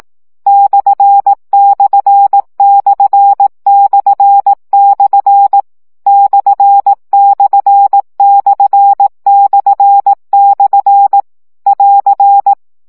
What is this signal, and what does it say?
Morse code, heard as a keyed tone, about 18 wpm, 800 Hz. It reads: ///// ///// <AR>